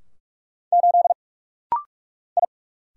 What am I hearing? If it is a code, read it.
Morse code, heard as a keyed tone, about 46 wpm, 700 Hz. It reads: 8 I
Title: Single Letter-Number - Rapid-Fire 45wpm